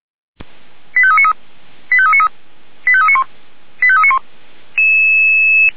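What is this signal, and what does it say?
Five-tone selective call (ZVEI), heard as quick runs of five tones, tones of about 70 ms. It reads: 85393 85393 85391 85391
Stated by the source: ZVEI five-tone selective call received with an SDR.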